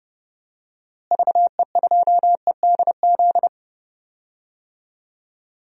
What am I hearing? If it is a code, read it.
Morse code, heard as a keyed tone, about 31 wpm, 700 Hz. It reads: VE2EDZ
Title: DXCC-100 Call Signs - 6 Characters - Repeated 3x - 30wpm